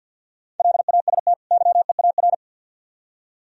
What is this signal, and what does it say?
Morse code, heard as a keyed tone, about 66 wpm, spaced as for 58 wpm, 700 Hz. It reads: LAST YEAR
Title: Speed-Racing - Sets of 2 Words 50wpm